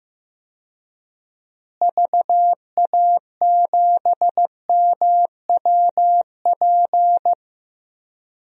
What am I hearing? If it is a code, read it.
Morse code, heard as a keyed tone, about 15 wpm, 700 Hz. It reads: VA7MWP
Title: DXCC-100 Call Signs - 6 Characters - Repeated 2x - 15wpm